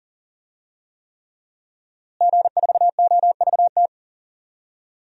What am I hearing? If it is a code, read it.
Morse code, heard as a keyed tone, about 37 wpm, 700 Hz. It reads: G4OVT
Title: DXCC-100 Call Signs - 5 Characters - Repeated 3x - 40wpm